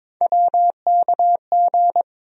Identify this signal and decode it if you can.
Morse code, heard as a keyed tone, about 22 wpm, 700 Hz. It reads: WKG
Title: USA Call Sign Postfixes - 3 Letters 22wpm